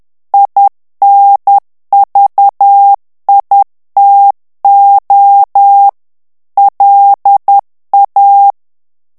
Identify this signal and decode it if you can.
Morse code, heard as a keyed tone, about 10 wpm, 800 Hz. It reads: INVITO LA
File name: morse mp3 / ITALIEN texte clair 1 vitesse 10 mots minute